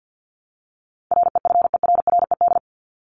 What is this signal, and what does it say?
Morse code, heard as a keyed tone, about 50 wpm, 700 Hz. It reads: REFERRED
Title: ED Suffix - Top 100 Words with Base Words 50wpm